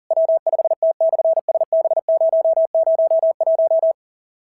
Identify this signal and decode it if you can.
Morse code, heard as a keyed tone, about 40 wpm, 690 Hz. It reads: W5T/SB001
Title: SOTA QSO Practice 40wpm